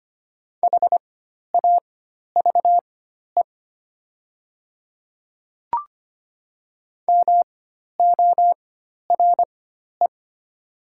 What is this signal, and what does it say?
Morse code, heard as a keyed tone, about 25 wpm, spaced as for 12 wpm, 700 Hz. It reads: HAVE MORE
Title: 4x Character Spacing - Top 100 Words - 25wpm